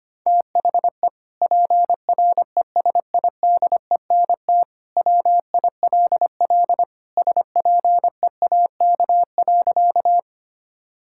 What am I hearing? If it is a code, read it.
Morse code, heard as a keyed tone, about 25 wpm, 700 Hz. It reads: THE PRESIDENT WILL SPEAK.
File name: practice / 25wpm / Sentences with 2-Word Phrases - 25wpm